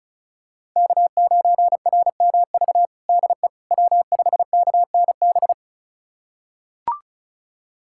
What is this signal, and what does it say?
Morse code, heard as a keyed tone, about 34 wpm, 700 Hz. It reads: K9RMV DE W5KNB
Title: USA Callsign DE Callsign - 5 Characters - 35wpm